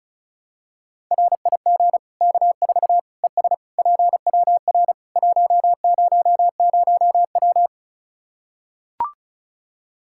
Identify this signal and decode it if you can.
Morse code, heard as a keyed tone, about 36 wpm, 700 Hz. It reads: RIG K4 ES PWR 100W
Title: Practice QSO - Exchange 2 - Info - 35wpm